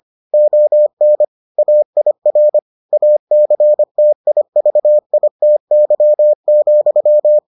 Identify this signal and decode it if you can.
Morse code, heard as a keyed tone, about 25 wpm, 600 Hz. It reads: ON AIR ACTIVITY,